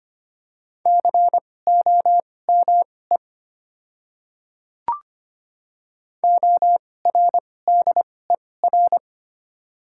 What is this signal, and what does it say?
Morse code, heard as a keyed tone, about 25 wpm, spaced as for 18 wpm, 700 Hz. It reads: COME ORDER
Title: 2x Character Spacing - Top 500 Words - 25wpm